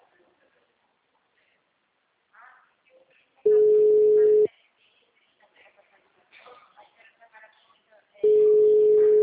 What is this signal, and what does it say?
Ringback tone (Europe, 425 Hz), with speech.